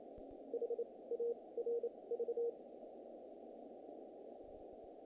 Morse code, heard as a keyed tone, about 28 wpm, spaced as for 20 wpm, 470 Hz. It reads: HARV
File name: contests / naqp-cw-k5tr-op-k5pi- -1800